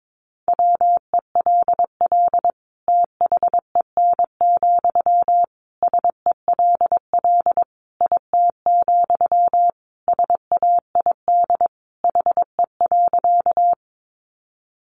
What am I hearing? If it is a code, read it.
Morse code, heard as a keyed tone, about 22 wpm, 700 Hz. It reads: WELL THEN, SELL IT, SAID HE.
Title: Sentences from Top 1100 Words 22wpm